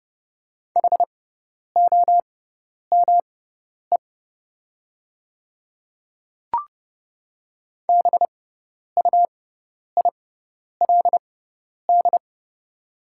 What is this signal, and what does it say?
Morse code, heard as a keyed tone, about 30 wpm, spaced as for 10 wpm, 700 Hz. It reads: HOME BUILD
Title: 6x Character Spacing - Top 200 Words - 30wpm